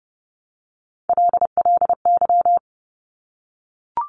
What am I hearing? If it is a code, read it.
Morse code, heard as a keyed tone, about 30 wpm, 690 Hz. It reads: LLY
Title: Top Three Letter Combinations in Words 30wpm